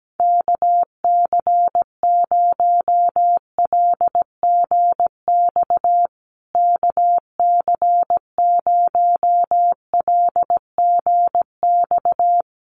Morse code, heard as a keyed tone, about 17 wpm, 700 Hz. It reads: KC0LGX KC0LGX
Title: USA Call Signs Repeated 2x 17wpm